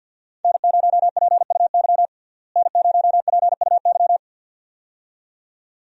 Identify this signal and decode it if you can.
Morse code, heard as a keyed tone, about 48 wpm, 700 Hz. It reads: N0PUY N0PUY
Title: USA Call Signs - 5 Characters - Repeated 2x - 50wpm